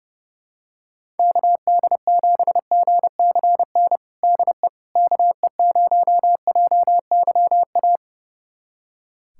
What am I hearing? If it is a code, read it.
Morse code, heard as a keyed tone, about 30 wpm, 700 Hz. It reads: KD7GCN DE KE0JYA
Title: USA Callsign DE Callsign 30wpm